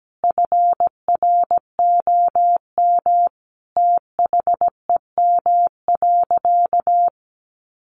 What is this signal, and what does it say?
Morse code, heard as a keyed tone, about 17 wpm, 700 Hz. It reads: FROM THEM.
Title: Sentences from Top 900 Words 17wpm